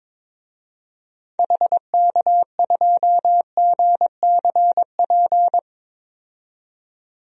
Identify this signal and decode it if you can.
Morse code, heard as a keyed tone, about 21 wpm, 700 Hz. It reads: HK2GCP
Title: DXCC-100 Call Signs - 6 Characters - 22wpm